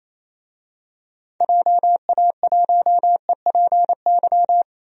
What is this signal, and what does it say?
Morse code, heard as a keyed tone, about 28 wpm, 700 Hz. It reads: JA1EPY
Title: DXCC-100 Call Signs - 6 Characters - Repeated 2x - 28wpm